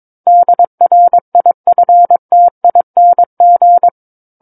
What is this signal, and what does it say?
Morse code, heard as a keyed tone, about 22 wpm, 700 Hz. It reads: DRIFTING
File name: Book One / 13-Chapter 13 - 22 WPM